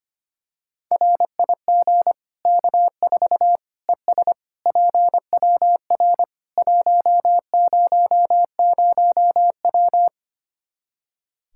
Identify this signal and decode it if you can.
Morse code, heard as a keyed tone, about 25 wpm, 700 Hz. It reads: RIG K4 ES PWR 100W
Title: Practice QSO - Exchange 2 - Info - 25wpm